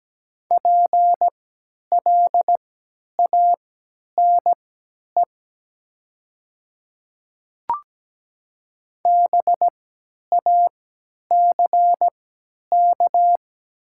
Morse code, heard as a keyed tone, about 17 wpm, spaced as for 10 wpm, 700 Hz. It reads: PLANE BACK
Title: 3x Character Spacing - Top 500 Words - 17wpm